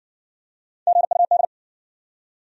Morse code, heard as a keyed tone, about 58 wpm, 700 Hz. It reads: GUD